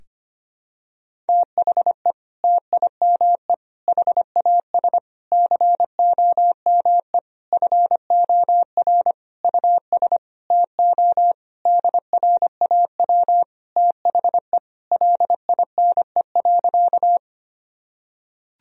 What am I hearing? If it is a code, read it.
Morse code, heard as a keyed tone, about 25 wpm, 700 Hz. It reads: THE TIME HAS COME FOR US TO DRAW THE LINE.
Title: Sentences from Top 1000 Words - Review Part 1 - 25wpm